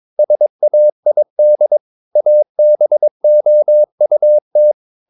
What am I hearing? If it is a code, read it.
Morse code, heard as a keyed tone, about 22 wpm, 600 Hz. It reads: SAID ABOUT
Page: CW med Gnister